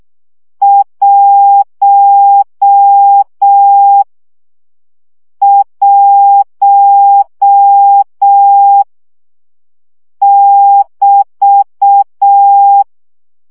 Morse code, heard as a keyed tone, about 6 wpm, 800 Hz. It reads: 1 1 <BT>
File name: CW gyakorlas 1 5 baud